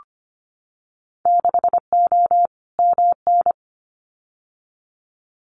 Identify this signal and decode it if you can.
Morse code, heard as a keyed tone, about 25 wpm, 700 Hz. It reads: 6O MN
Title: Winter Field Day Exchanges Encore 25wpm